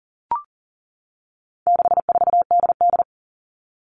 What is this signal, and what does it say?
Morse code, heard as a keyed tone, about 40 wpm, 700 Hz. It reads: 64DD
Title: All Cut Numbers - 4-Digit 40wpm